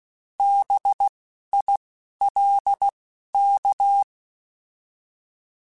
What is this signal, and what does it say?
Morse code, heard as a keyed tone, about 16 wpm, 800 Hz. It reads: B I L K